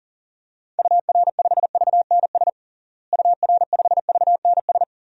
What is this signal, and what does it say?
Morse code, heard as a keyed tone, about 40 wpm, 700 Hz. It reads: UR5VNS UR5VNS
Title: DXCC-100 Call Signs - 6 Characters - Repeated 2x - 40wpm